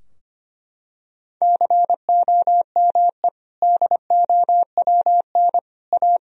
Morse code, heard as a keyed tone, about 25 wpm, 700 Hz. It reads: COME DOWN A
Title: Sentences from Top 1000 Words - Review Part 2 - 25wpm